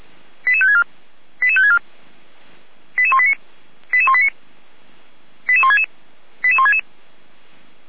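Five-tone selective call (ZVEI), heard as quick runs of five tones, tones of about 80 ms. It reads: 88564 88564 88189 88189 88166 88166
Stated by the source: Five-tone ZVEI selective call 88169, received on an SDR; from the feuerwehr-orlamuende.org fire brigade audio server.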